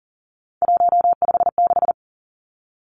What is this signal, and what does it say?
Morse code, heard as a keyed tone, about 40 wpm, 700 Hz. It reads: 156
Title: Numbers - 3-Digit 40wpm